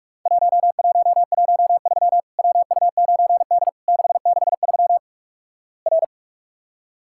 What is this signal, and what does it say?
Morse code, heard as a keyed tone, about 46 wpm, 700 Hz. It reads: 1113 WU9D 663 R
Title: Worked All Europe DX Contest - QTC Exchange - 45wpm